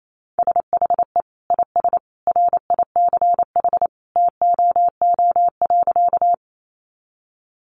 Morse code, heard as a keyed tone, about 28 wpm, 700 Hz. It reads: SHE IS RICH TOO.
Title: Sentences from Top 1100 Words 28wpm